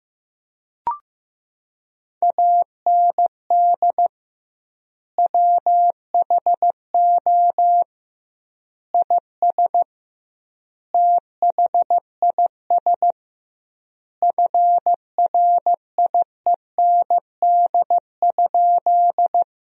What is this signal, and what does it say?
Morse code, heard as a keyed tone, about 15 wpm, 700 Hz. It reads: AND WHO IS THIS FRIEND?
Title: 2x Word-Spacing - Sentences from Top 300 Words - 15wpm